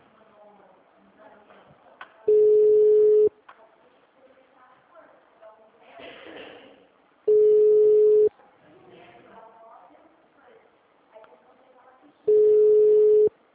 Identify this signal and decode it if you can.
Ringback tone (Europe, 425 Hz), with speech.